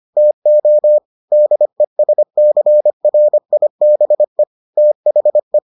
Morse code, heard as a keyed tone, about 25 wpm, 600 Hz. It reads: TO DESCRIBE THE